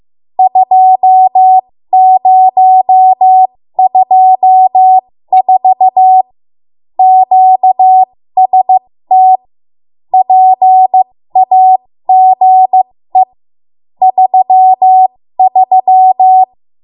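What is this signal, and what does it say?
Morse code, heard as a keyed tone, about 15 wpm, spaced as for 13 wpm, 750 Hz. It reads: 2024 QST PAGE 33
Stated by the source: 13 wpm Morse code practice archive transmission.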